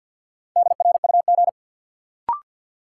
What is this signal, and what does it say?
Morse code, heard as a keyed tone, about 53 wpm, 700 Hz. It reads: DRUG
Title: Sets of 5 Words - New Word Drill - 50wpm